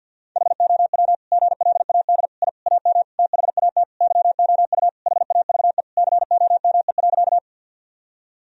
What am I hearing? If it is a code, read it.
Morse code, heard as a keyed tone, about 52 wpm, 700 Hz. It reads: HOW GLAD I AM THAT YOU HAVE COME.
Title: Sentences from Top 1000 Words - Review Part 2 - 50wpm